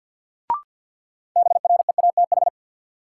Morse code, heard as a keyed tone, about 52 wpm, 700 Hz. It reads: BREATH
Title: Sets of 5 Words Encore - New Word Drill - 50wpm